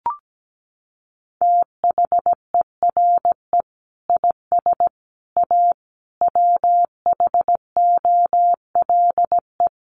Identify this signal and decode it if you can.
Morse code, heard as a keyed tone, about 17 wpm, 700 Hz. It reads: THERE IS A WHOLE
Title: Sentences from Top 1400 Words 17wpm